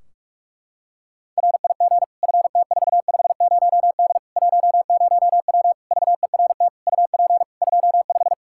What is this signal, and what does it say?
Morse code, heard as a keyed tone, about 41 wpm, 700 Hz. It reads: RIG FT450D 10W VERT UP 25
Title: Practice QSO - Exchange 2 - Info - 45wpm